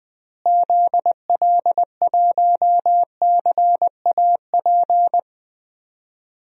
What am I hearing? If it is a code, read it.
Morse code, heard as a keyed tone, about 20 wpm, 700 Hz. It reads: ZL1CAP
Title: DXCC-100 Call Signs - 6 Characters - Repeated 2x - 20wpm